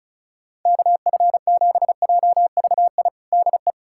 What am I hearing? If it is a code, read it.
Morse code, heard as a keyed tone, about 36 wpm, 700 Hz. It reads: KF7JVI DE
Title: USA Callsign DE Callsign - 6 Characters - 35wpm